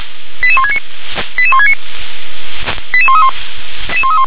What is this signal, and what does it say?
Five-tone selective call (ZVEI), heard as quick runs of five tones, tones of about 70 ms. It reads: 88169 88169 88131